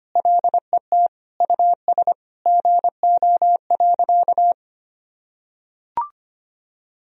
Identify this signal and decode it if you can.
Morse code, heard as a keyed tone, about 25 wpm, 700 Hz. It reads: LET US GO.
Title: Sentences from Top 1000 Words - Review Part 2 - 25wpm